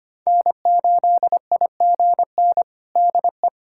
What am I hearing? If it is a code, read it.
Morse code, heard as a keyed tone, about 25 wpm, 700 Hz. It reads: N8IGN DE
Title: USA Callsign DE Callsign - 5 Characters - 25wpm